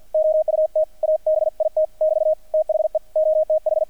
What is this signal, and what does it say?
Morse code, heard as a keyed tone, about 50 wpm, 620 Hz. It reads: OUT ABIT <BT> THE OTH
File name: archived / CRSnet